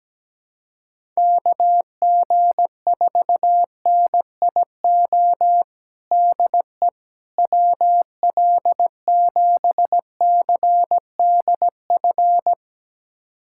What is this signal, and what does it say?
Morse code, heard as a keyed tone, about 17 wpm, 700 Hz. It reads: KG4NIO DE WL7CDF
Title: USA Callsign DE Callsign - 6 Characters - 17wpm